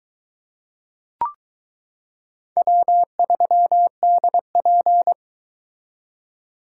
Morse code, heard as a keyed tone, about 22 wpm, 700 Hz. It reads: W3DP